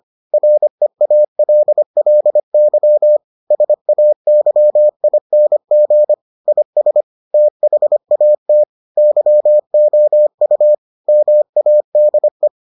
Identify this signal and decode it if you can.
Morse code, heard as a keyed tone, about 25 wpm, 600 Hz. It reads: REALLY SAYING IS THAT YOU MADE